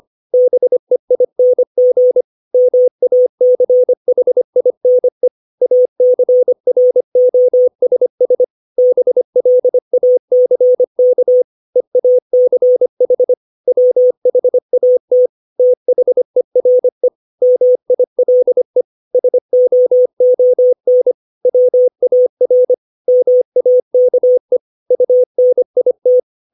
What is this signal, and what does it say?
Morse code, heard as a keyed tone, about 25 wpm, 500 Hz. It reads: BEING MACHINE ACROSS BLACK EACH WHAT THERE MILE SOON WAR MAKE UNIT
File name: 632 words / 632 Words 25WPM